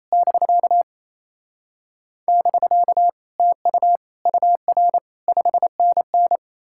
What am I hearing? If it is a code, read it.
Morse code, heard as a keyed tone, about 37 wpm, 700 Hz. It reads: <BK> <BK> TU UR 5NN